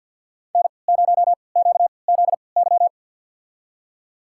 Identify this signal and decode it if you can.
Morse code, heard as a keyed tone, about 52 wpm, spaced as for 29 wpm, 700 Hz. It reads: N0QZY
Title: USA Call Signs - 5 Characters - 3x Character Spacing - 50wpm